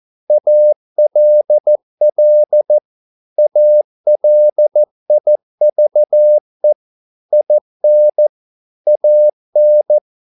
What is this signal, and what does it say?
Morse code, heard as a keyed tone, about 14 wpm, 600 Hz. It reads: ALL ALIVE IN AN